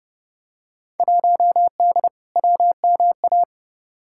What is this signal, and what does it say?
Morse code, heard as a keyed tone, about 30 wpm, 700 Hz. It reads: 1D WMA
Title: Field Day Exchanges - 30wpm